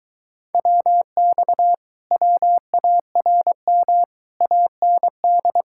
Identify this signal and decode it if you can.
Morse code, heard as a keyed tone, about 23 wpm, 700 Hz. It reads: WX WARM AND